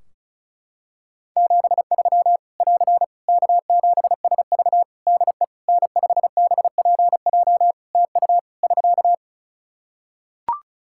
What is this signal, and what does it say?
Morse code, heard as a keyed tone, about 34 wpm, 700 Hz. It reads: 73 <AR> K7SV DE N5BPJ TU <SK>
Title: Practice QSO - Exchange 4 - Ending - 35wpm